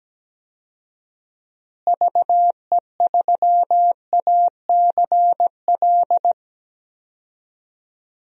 Morse code, heard as a keyed tone, about 17 wpm, 700 Hz. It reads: VE3ACL
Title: DXCC-100 Call Signs - 6 Characters - Repeated 3x - 17wpm